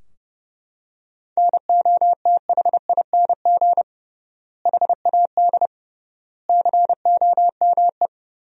Morse code, heard as a keyed tone, about 30 wpm, 700 Hz. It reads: NOTHING HAD COME
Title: 3x Word-Spacing - Sentences from Top 500 Words - 30wpm